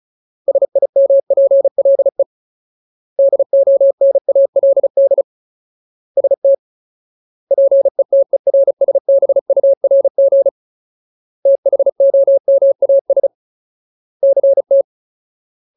Morse code, heard as a keyed tone, about 36 wpm, 550 Hz. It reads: SIMPLE DONALD ST PETERSBURG THOMAS CT
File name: Ultimate Word List / 35WPM Ultimate